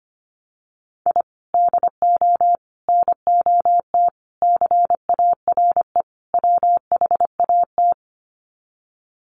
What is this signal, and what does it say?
Morse code, heard as a keyed tone, about 25 wpm, 700 Hz. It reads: I DO NOT CARE WHAT
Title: Sets of 5 Words 25wpm